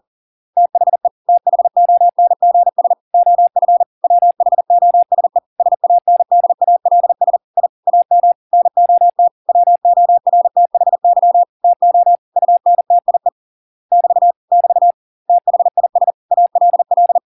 Morse code, heard as a keyed tone, about 40 wpm, 700 Hz. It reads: THE THONGS OF WHOSE SANDALS I AM NOT WORTHY TO UNTIE <BT> <BT> THIS ALL